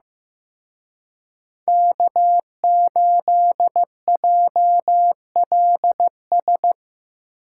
Morse code, heard as a keyed tone, about 17 wpm, 700 Hz. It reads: K8JLS
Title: Speed Racing - USA Call Signs 15wpm